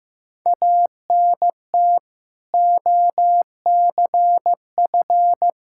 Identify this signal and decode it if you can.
Morse code, heard as a keyed tone, about 15 wpm, 700 Hz. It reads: ANT OCF
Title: Practice QSO - Exchange 2 - Info - 15wpm